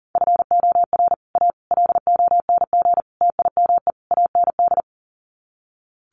Morse code, heard as a keyed tone, about 40 wpm, 700 Hz. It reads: FOR A LONG TIME AND
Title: Sets of 5 Words Encore2 40wpm